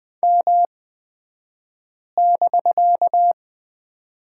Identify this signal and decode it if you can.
Morse code, heard as a keyed tone, about 20 wpm, spaced as for 5 wpm, 700 Hz. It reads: M<BK>
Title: 249 Lesson 40d - BK - Two Characters 20wpm